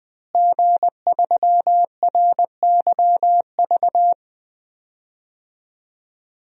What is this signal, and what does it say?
Morse code, heard as a keyed tone, about 20 wpm, 700 Hz. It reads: G3RYV
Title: DXCC-100 Call Signs - 5 Characters - Repeated 2x - 20wpm